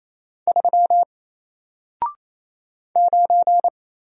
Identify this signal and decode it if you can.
Morse code, heard as a keyed tone, about 27 wpm, 700 Hz. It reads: 3 9
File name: Single-Letter-Number-28wpm